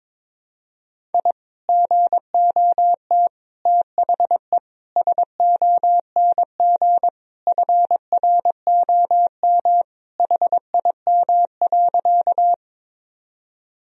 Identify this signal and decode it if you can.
Morse code, heard as a keyed tone, about 22 wpm, 700 Hz. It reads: I GOT THE SONG FROM HIM.
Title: Sentences from Top 1000 Words - Review Part 2 - 22wpm